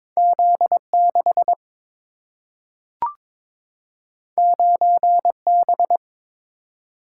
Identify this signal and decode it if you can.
Morse code, heard as a keyed tone, about 22 wpm, 700 Hz. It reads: Z6 9B
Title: Single-Letter Single-Number 22wpm